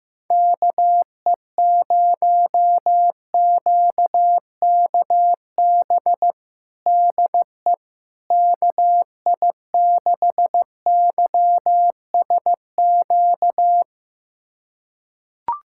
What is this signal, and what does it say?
Morse code, heard as a keyed tone, about 15 wpm, 700 Hz. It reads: KE0QKB DE KI6YSQ
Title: USA Callsign DE Callsign - 6 Characters - 15wpm